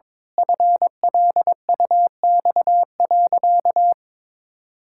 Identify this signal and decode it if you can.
Morse code, heard as a keyed tone, about 22 wpm, 700 Hz. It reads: FLUX.
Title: Pangram Challenge v2 - 22wpm